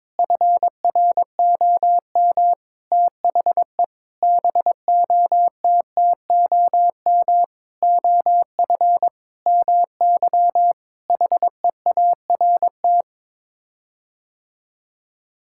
Morse code, heard as a keyed tone, about 22 wpm, 700 Hz. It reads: FROM THE BOTTOM OF MY HEART
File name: practice / 22wpm / Sets of 8 Words - 22wpm